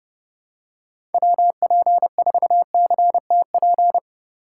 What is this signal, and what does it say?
Morse code, heard as a keyed tone, about 31 wpm, 700 Hz. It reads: WP4CTP
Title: USA Call Signs - 6 Characters - 4x Character Spacing - 30wpm